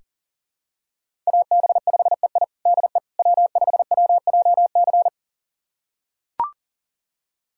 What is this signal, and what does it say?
Morse code, heard as a keyed tone, about 40 wpm, 700 Hz. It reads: AB5EI DE W5WJC
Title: USA Callsign DE Callsign - 5 Characters - 40wpm